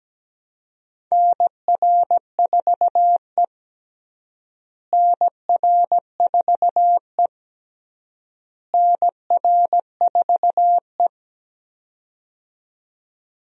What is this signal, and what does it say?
Morse code, heard as a keyed tone, about 17 wpm, 700 Hz. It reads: NR4E NR4E NR4E
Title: USA Call Signs - 4 Characters - Repeated 3x - 17wpm